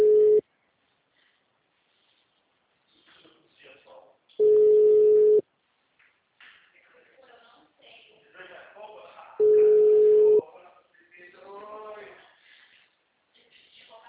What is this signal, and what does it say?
Ringback tone (Europe, 425 Hz), with speech.